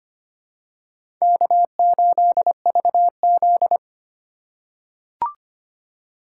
Morse code, heard as a keyed tone, about 25 wpm, 700 Hz. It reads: K8VZ
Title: USA Call Signs - 2x Character Spacing - 25wpm